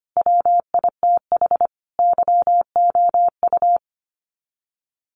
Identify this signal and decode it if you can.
Morse code, heard as a keyed tone, about 25 wpm, 700 Hz. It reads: WITH YOU
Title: Sets of 5 Words Encore 25wpm